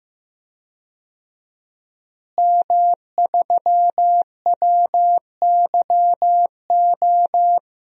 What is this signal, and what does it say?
Morse code, heard as a keyed tone, about 15 wpm, 700 Hz. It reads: M3WYO
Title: DXCC-100 Call Signs - 5 Characters - Repeated 3x - 15wpm